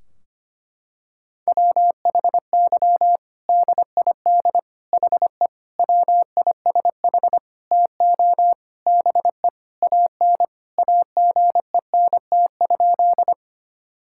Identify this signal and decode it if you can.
Morse code, heard as a keyed tone, about 25 wpm, 700 Hz. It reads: WHY DID HE WISH TO BE AN AGENT?
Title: Sentences from Top 1600 Words 25wpm